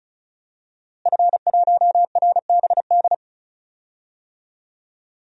Morse code, heard as a keyed tone, about 34 wpm, 700 Hz. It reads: F1RBD
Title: DXCC-100 Call Signs - 5 Characters - Repeated 2x - 35wpm